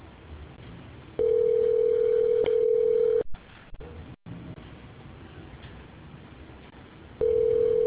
Ringback tone (North America), 440 + 480 Hz.